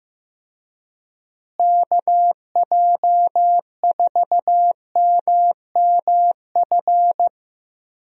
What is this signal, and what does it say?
Morse code, heard as a keyed tone, about 16 wpm, 700 Hz. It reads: KJ4MMF
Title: USA Call Signs - 6 Characters - Repeated 3x - 15wpm